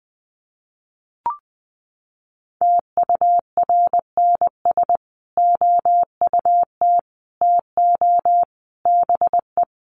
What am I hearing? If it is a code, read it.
Morse code, heard as a keyed tone, about 20 wpm, 700 Hz. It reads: TURNS OUT TO BE
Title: Sets of 5 Words Encore 20wpm